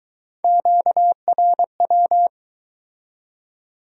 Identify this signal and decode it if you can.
Morse code, heard as a keyed tone, about 24 wpm, 700 Hz. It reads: QRW